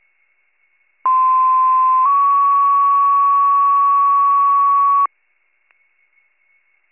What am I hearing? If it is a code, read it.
Two-tone page, heard as a one-second tone, then a three-second tone, about 1040 then 1120 Hz. It is heard from 1.1 to 5.1 s.